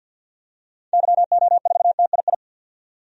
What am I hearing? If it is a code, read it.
Morse code, heard as a keyed tone, about 52 wpm, 700 Hz. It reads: YO4TII
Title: DXCC-100 Call Signs - 6 Characters - 50wpm